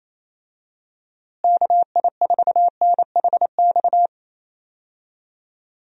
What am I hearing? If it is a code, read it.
Morse code, heard as a keyed tone, about 28 wpm, 700 Hz. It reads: KI4NHX
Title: USA Call Signs - 6 Characters - Repeated 2x - 28wpm